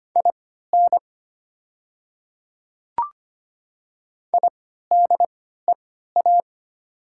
Morse code, heard as a keyed tone, about 25 wpm, spaced as for 14 wpm, 700 Hz. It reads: IN IDEA
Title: 3x Character Spacing - Top 300 Words - 25wpm